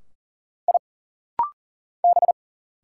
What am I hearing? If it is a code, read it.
Morse code, heard as a keyed tone, about 40 wpm, 700 Hz. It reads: I B
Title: Single Letters - Rapid-Fire 40wpm